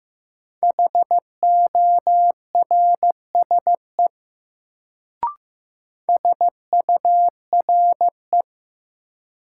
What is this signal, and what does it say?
Morse code, heard as a keyed tone, about 15 wpm, 700 Hz. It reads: HORSE SURE
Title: Sentences from Top 300 Words - New Word Drill - 15wpm